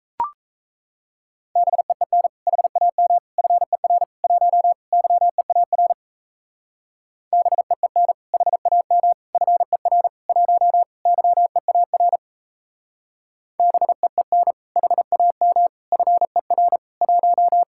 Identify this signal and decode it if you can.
Morse code, heard as a keyed tone, about 36 wpm, 700 Hz. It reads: BEEN HAM FER 1 YEAR BEEN HAM FER 1 YEAR BEEN HAM FER 1